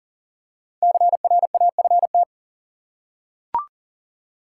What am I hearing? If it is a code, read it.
Morse code, heard as a keyed tone, about 40 wpm, 700 Hz. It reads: CRAFT
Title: Five Letter Words - 40wpm